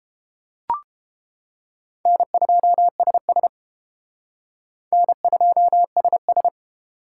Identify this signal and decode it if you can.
Morse code, heard as a keyed tone, about 30 wpm, 700 Hz. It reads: N2SS N2SS